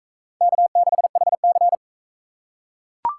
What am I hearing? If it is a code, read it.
Morse code, heard as a keyed tone, about 42 wpm, 700 Hz. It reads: K6HC